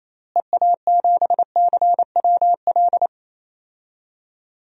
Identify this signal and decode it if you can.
Morse code, heard as a keyed tone, about 28 wpm, 700 Hz. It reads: EA7CWL
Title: DXCC-100 Call Signs - 6 Characters - Repeated 3x - 28wpm